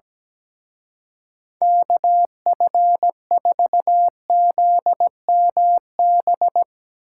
Morse code, heard as a keyed tone, about 17 wpm, 700 Hz. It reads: KF4ZMB
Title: Speed Racing - USA 2x3 Call Signs 17wpm